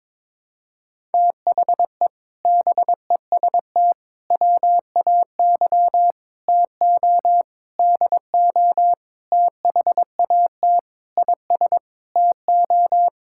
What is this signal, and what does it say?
Morse code, heard as a keyed tone, about 22 wpm, 700 Hz. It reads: THE BEST WAY TO DO THAT IS TO
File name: practice / 22wpm / Sentences with 8-Word Phrases - 22wpm